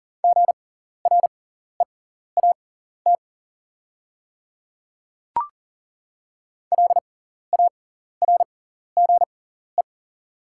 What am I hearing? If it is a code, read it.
Morse code, heard as a keyed tone, about 39 wpm, spaced as for 14 wpm, 700 Hz. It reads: GREAT LARGE
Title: 6x Character Spacing - Top 200 Words - 40wpm